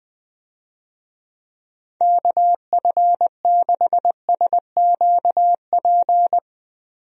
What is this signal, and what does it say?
Morse code, heard as a keyed tone, about 20 wpm, 700 Hz. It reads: KF6SQP
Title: USA Call Signs - 6 Characters - Repeated 2x - 20wpm